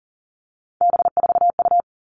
Morse code, heard as a keyed tone, about 40 wpm, 700 Hz. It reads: B4U
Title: All Cut Numbers - 3-Digit 40wpm